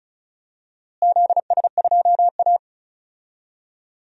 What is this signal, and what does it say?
Morse code, heard as a keyed tone, about 36 wpm, 700 Hz. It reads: ZS2A
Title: DXCC-100 Call Signs - 4 Characters - Repeated 2x - 35wpm